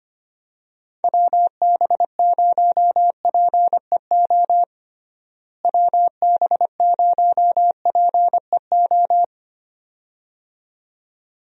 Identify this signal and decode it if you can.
Morse code, heard as a keyed tone, about 25 wpm, 700 Hz. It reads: WB0PEO WB0PEO
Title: USA Call Signs - 6 Characters - Repeated 2x - 25wpm